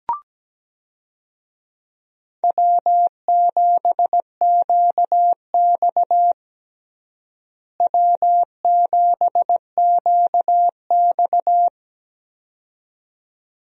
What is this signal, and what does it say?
Morse code, heard as a keyed tone, about 17 wpm, 700 Hz. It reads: W7QX W7QX
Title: USA Call Signs - 4 Characters - Repeated 2x - 17wpm